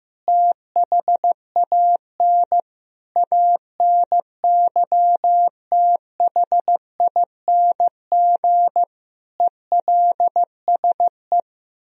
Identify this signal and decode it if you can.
Morse code, heard as a keyed tone, about 15 wpm, 700 Hz. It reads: THAN ANYTHING ELSE
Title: Sentences from Top 1000 Words - Review Part 4 - 15wpm